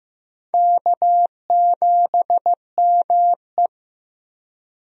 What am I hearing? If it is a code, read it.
Morse code, heard as a keyed tone, about 16 wpm, 700 Hz. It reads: K7ME
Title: USA Call Signs - 4 Characters - 15wpm